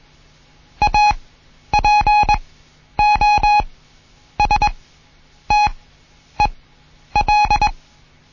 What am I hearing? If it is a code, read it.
Morse code, heard as a keyed tone, about 21 wpm, spaced as for 10 wpm, 840 Hz. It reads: APOSTEL